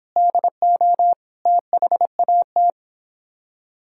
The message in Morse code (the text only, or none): DO THAT